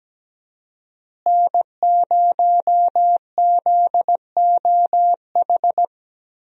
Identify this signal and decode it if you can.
Morse code, heard as a keyed tone, about 17 wpm, 700 Hz. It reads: N0ZOH
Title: USA Call Signs - 5 Characters - Repeated 2x - 17wpm